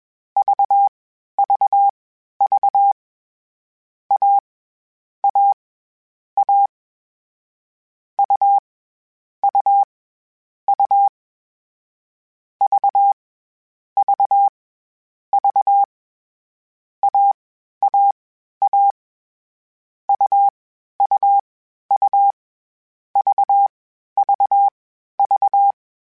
Morse code, heard as a keyed tone, about 21 wpm, spaced as for 12 wpm, 800 Hz. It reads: VVV A A A U U U V V V AAA UUU VVV
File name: Difficult Combos / Difficult Combos-mp3s / AUV ShortRun @20